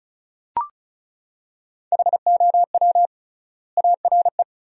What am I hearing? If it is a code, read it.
Morse code, heard as a keyed tone, about 36 wpm, 700 Hz. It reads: HOW ARE
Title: 3x Word-Spacing - Sentences from Top 100 Words - 35wpm